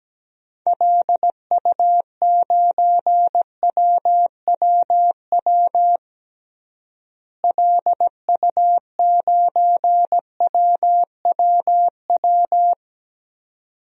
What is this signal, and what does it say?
Morse code, heard as a keyed tone, about 17 wpm, 700 Hz. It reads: LU9WWW LU9WWW
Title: DXCC-100 Call Signs - 6 Characters - Repeated 2x - 17wpm